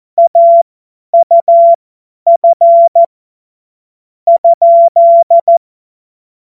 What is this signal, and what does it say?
Morse code, heard as a keyed tone, about 14 wpm, spaced as for 8 wpm, 680 Hz. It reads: AUF?
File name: Morsekurs / Witze / Witz 30. August 2022 50 F70